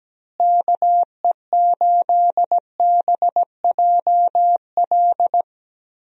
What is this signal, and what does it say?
Morse code, heard as a keyed tone, about 17 wpm, 700 Hz. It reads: KE8BJL
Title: USA Call Signs - 6 Characters - Repeated 2x - 17wpm